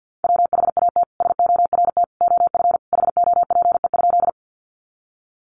Morse code, heard as a keyed tone, about 50 wpm, 700 Hz. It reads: WHAT SORT OF HOPE?
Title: Sentences from Top 1300 Words 50wpm